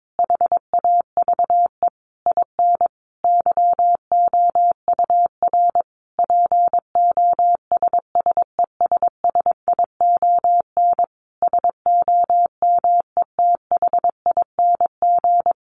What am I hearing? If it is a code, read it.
Morse code, heard as a keyed tone, about 22 wpm, 700 Hz. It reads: HAVE IN YOUR POSSESSION SOMETHING
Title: Sentences from Top 1400 Words 22wpm